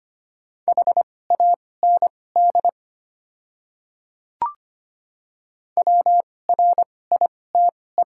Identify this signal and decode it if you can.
Morse code, heard as a keyed tone, about 25 wpm, spaced as for 18 wpm, 690 Hz. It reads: HAND WRITE
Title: 2x Character Spacing - Top 500 Words - 25wpm